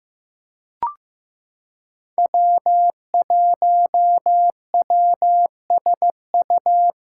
Morse code, heard as a keyed tone, about 15 wpm, 700 Hz. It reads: W1WSU
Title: USA Callsign DE Callsign - 5 Characters - 15wpm